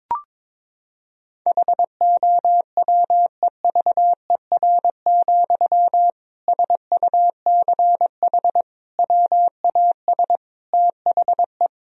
Morse code, heard as a keyed tone, about 22 wpm, 700 Hz. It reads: HOWEVER, SUCH WAS THE
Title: Sentences from Top 1000 Words - Review Part 3 - 22wpm